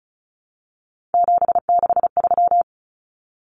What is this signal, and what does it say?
Morse code, heard as a keyed tone, about 34 wpm, 700 Hz. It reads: 763
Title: Numbers - 3-Digit 35wpm